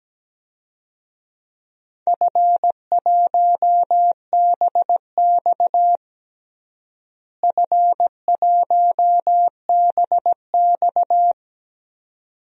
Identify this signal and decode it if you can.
Morse code, heard as a keyed tone, about 17 wpm, 700 Hz. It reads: F1BX F1BX
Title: DXCC-100 Call Signs - 4 Characters - Repeated 2x - 17wpm